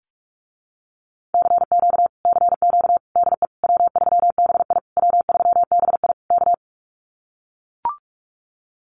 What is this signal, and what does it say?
Morse code, heard as a keyed tone, about 44 wpm, 700 Hz. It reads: CQ CQ DE W3BI W3BI K
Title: Common QSO Phrases Encore 45wpm